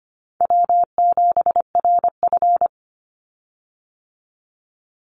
Morse code, heard as a keyed tone, about 25 wpm, 700 Hz. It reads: W7RF
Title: Easy Call Signs Repeated 3x - 25wpm